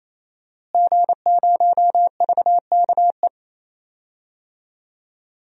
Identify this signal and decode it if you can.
Morse code, heard as a keyed tone, about 28 wpm, 690 Hz. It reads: G0VKE
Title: DXCC-100 Call Signs - 5 Characters - Repeated 3x - 28wpm